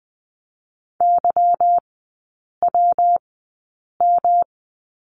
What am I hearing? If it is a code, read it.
Morse code, heard as a keyed tone, about 20 wpm, spaced as for 8 wpm, 700 Hz. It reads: YWM
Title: 166 Lesson 27e - Number 7 - Three Characters 20wpm